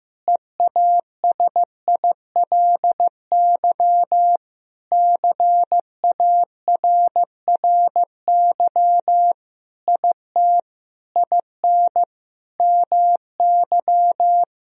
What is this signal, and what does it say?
Morse code, heard as a keyed tone, about 16 wpm, 700 Hz. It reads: EASILY CARRY IT IN MY
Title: Sentences from Top 1000 Words - Review Part 1 - 15wpm